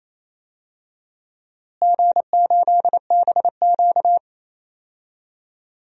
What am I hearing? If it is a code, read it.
Morse code, heard as a keyed tone, about 28 wpm, 700 Hz. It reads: G8BQ
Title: DXCC-100 Call Signs - 4 Characters - Repeated 2x - 28wpm